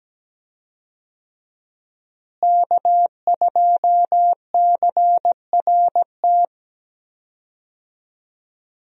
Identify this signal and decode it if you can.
Morse code, heard as a keyed tone, about 17 wpm, 700 Hz. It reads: K2CRT
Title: USA Call Signs - 5 Characters - Repeated 3x - 17wpm